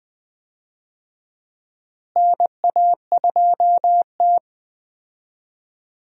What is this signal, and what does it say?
Morse code, heard as a keyed tone, about 20 wpm, 700 Hz. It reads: NA2T
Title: USA Call Signs - 4 Characters - Repeated 2x - 20wpm